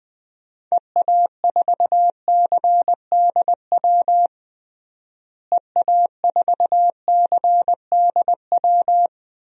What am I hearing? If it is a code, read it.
Morse code, heard as a keyed tone, about 20 wpm, 700 Hz. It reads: EA4CDW EA4CDW
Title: DXCC-100 Call Signs - 6 Characters - Repeated 3x - 20wpm